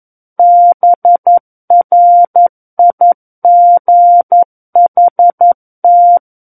Morse code, heard as a keyed tone, about 11 wpm, 700 Hz. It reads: BRIGHT
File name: Book One / 02-Chapter 2 - 11 WPM